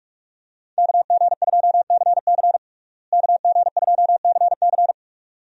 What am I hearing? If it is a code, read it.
Morse code, heard as a keyed tone, about 46 wpm, 700 Hz. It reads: KG2CC KG2CC
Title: USA Call Signs - 5 Characters - Repeated 2x - 45wpm